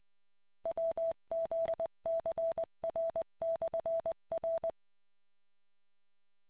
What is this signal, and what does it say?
Morse code, heard as a keyed tone, about 20 wpm, 660 Hz. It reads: WZCR/R